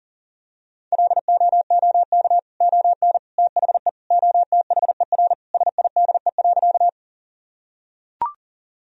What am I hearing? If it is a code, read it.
Morse code, heard as a keyed tone, about 40 wpm, 700 Hz. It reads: LOOK ON THE OTHER SIDE.
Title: Sentences from Top 1000 Words - Review Part 1 - 40wpm